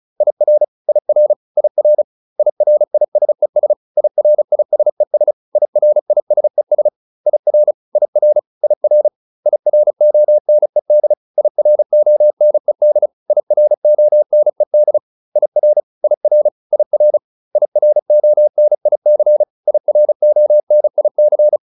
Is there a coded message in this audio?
Yes: Morse code from 0.2 to 21.6 s, heard as a keyed tone, about 34 wpm, 600 Hz. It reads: IR IR IR IRISES IRISES IRISES IR IR IR IRONED IRONED IRONED IR IR IR IRONIC IRONIC